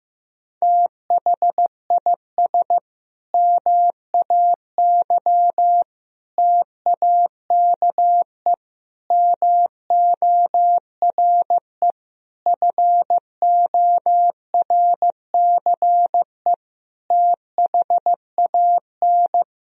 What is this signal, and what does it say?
Morse code, heard as a keyed tone, about 15 wpm, 700 Hz. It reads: THIS MAY TAKE MORE FORCE THAN
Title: Sentences from Top 1000 Words - Review Part 3 - 15wpm